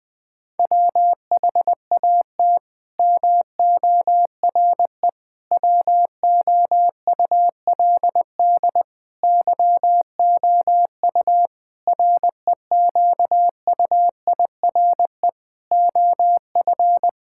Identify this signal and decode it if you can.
Morse code, heard as a keyed tone, about 20 wpm, 700 Hz. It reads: WHAT MORE WOULD YOU REQUIRE OF
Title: Sentences from Top 1000 Words - Review Part 6 - 20wpm